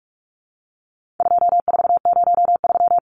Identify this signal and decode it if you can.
Morse code, heard as a keyed tone, about 44 wpm, 700 Hz. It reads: 2403
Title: Numbers - 4-Digit 45wpm